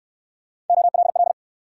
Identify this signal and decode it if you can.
Morse code, heard as a keyed tone, about 59 wpm, 700 Hz. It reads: OLL